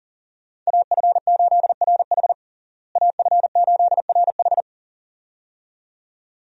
Morse code, heard as a keyed tone, about 37 wpm, 700 Hz. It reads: AF8RH AF8RH
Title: USA Call Signs - 5 Characters - Repeated 2x - 40wpm